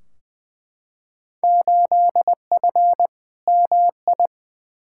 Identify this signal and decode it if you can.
Morse code, heard as a keyed tone, about 20 wpm, 700 Hz. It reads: 8F MI
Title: Field Day Exchanges Encore - 20wpm